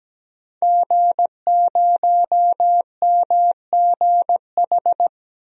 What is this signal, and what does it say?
Morse code, heard as a keyed tone, about 17 wpm, 700 Hz. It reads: G0MGH
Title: DXCC-100 Call Signs - 5 Characters - Repeated 2x - 17wpm